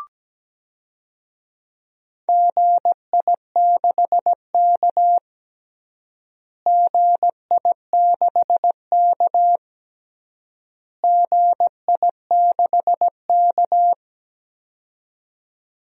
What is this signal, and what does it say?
Morse code, heard as a keyed tone, about 17 wpm, 700 Hz. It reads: GI6K GI6K GI6K
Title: DXCC-100 Call Signs - 4 Characters - Repeated 3x - 17wpm